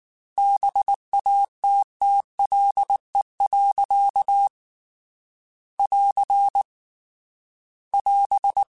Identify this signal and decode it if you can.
Morse code, heard as a keyed tone, about 19 wpm, 800 Hz. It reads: BATTLE. <AR> <AS>